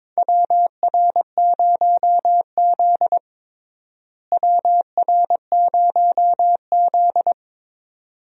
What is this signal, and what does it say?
Morse code, heard as a keyed tone, about 22 wpm, 700 Hz. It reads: WR0Z WR0Z
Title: USA Call Signs - 4 Characters - Repeated 2x - 22wpm